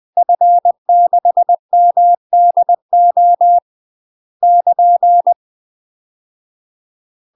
Morse code, heard as a keyed tone, about 20 wpm, 700 Hz. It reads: F6MDO <KN>
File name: UK MASTER QSOs TRAINING / 20WPM / UK MASTER QSOs 0052